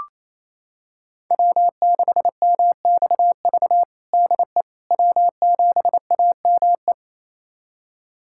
Morse code, heard as a keyed tone, about 27 wpm, 700 Hz. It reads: W6MXV DE W7AME
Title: USA Callsign DE Callsign - 5 Characters - 28wpm